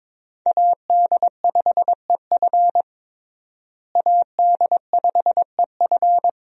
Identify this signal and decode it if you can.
Morse code, heard as a keyed tone, about 22 wpm, 700 Hz. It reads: AD5EF AD5EF
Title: USA Call Signs - 5 Characters - Repeated 2x - 22wpm